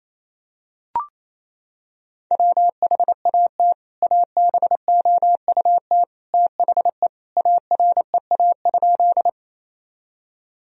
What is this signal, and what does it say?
Morse code, heard as a keyed tone, about 28 wpm, 700 Hz. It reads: WHAT ABOUT THE AREA?
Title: Sentences from Top 1000 Words - Review Part 2 - 28wpm